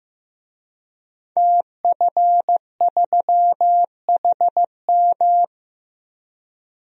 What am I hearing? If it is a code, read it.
Morse code, heard as a keyed tone, about 15 wpm, 700 Hz. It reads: TF3HM
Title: DXCC-100 Call Signs - 5 Characters - 15wpm